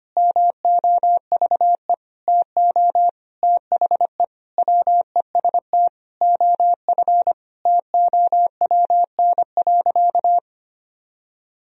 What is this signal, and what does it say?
Morse code, heard as a keyed tone, about 25 wpm, 700 Hz. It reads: MOVE TO THE WEST OF TOWN.
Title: Sentences from Top 1000 Words - Review Part 2 - 25wpm